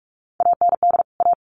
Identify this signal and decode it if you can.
Morse code, heard as a keyed tone, about 44 wpm, 700 Hz. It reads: AND A
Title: Sets of 2 Words 45wpm